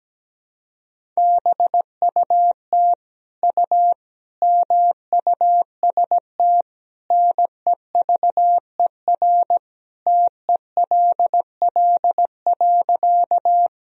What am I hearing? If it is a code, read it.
Morse code, heard as a keyed tone, about 17 wpm, 700 Hz. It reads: BUT U MUST NEVER TELL.
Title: Sentences from Top 300 Words and Abbreviations 17wpm